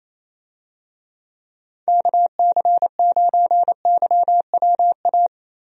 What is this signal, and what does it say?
Morse code, heard as a keyed tone, about 28 wpm, 700 Hz. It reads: KC9YWA
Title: USA Call Signs - 6 Characters - Repeated 2x - 28wpm